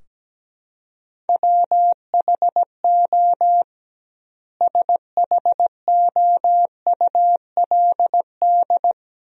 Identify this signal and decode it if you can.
Morse code, heard as a keyed tone, about 17 wpm, 700 Hz. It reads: WHO SHOULD
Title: 2x Word-Spacing - Sentences from Top 300 Words - 17wpm